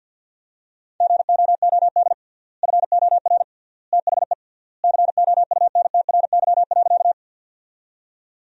Morse code, heard as a keyed tone, about 52 wpm, 700 Hz. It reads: GOOD FOR THE COUNTRY.
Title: 3x Word-Spacing - Sentences from Top 300 Words - 50wpm